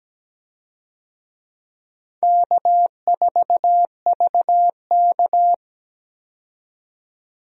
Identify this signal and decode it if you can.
Morse code, heard as a keyed tone, about 17 wpm, 700 Hz. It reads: K4VK
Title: USA Call Signs - 4 Characters - 17wpm